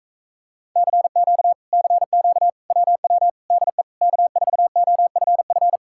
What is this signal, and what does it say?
Morse code, heard as a keyed tone, about 38 wpm, 700 Hz. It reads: CQ CQ WW DE K4OFF